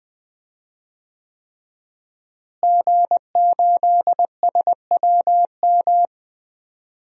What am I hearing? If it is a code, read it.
Morse code, heard as a keyed tone, about 20 wpm, 700 Hz. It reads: G8SWM
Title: DXCC-100 Call Signs - 5 Characters - Repeated 3x - 20wpm